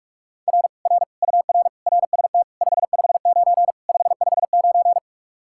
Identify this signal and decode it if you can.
Morse code, heard as a keyed tone, about 54 wpm, 700 Hz. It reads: R R UR RST 559 559
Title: Speed-Racing Common QSO Phrases - 45wpm